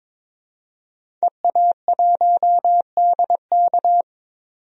Morse code, heard as a keyed tone, about 22 wpm, 700 Hz. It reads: EA1DK
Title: DXCC-100 Call Signs - 5 Characters - 22wpm